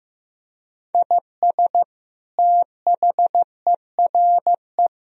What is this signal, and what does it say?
Morse code, heard as a keyed tone, about 16 wpm, 700 Hz. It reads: IS THERE
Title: Sentences from Top 1000 Words - Review Part 3 - 15wpm